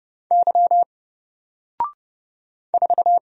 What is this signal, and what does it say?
Morse code, heard as a keyed tone, about 30 wpm, 700 Hz. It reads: Y 4
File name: Single-Letter-Number-30wpm